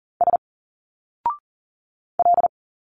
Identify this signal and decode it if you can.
Morse code, heard as a keyed tone, about 40 wpm, 700 Hz. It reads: S L
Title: Single Letters 40wpm